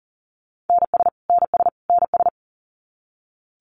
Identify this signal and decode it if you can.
Morse code, heard as a keyed tone, about 40 wpm, 700 Hz. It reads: NS NS NS
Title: Canadian Provinces and Territories - Repeated 3x - 40wpm